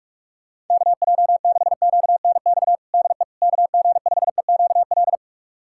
Morse code, heard as a keyed tone, about 46 wpm, 700 Hz. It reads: KJ6QNX DE KG5EQL
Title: USA Callsign DE Callsign - 6 Characters - 45wpm